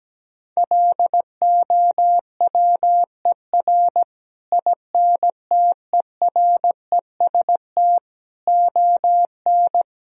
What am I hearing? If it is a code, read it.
Morse code, heard as a keyed tone, about 17 wpm, 700 Hz. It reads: LOWER INTEREST ON
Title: Sentences from Top 1000 Words - Review Part 4 - 17wpm